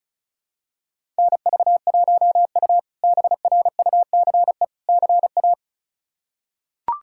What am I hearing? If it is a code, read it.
Morse code, heard as a keyed tone, about 36 wpm, 700 Hz. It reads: NV1U BRUCE CA
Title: K1USN SST QSO Practice - 35wpm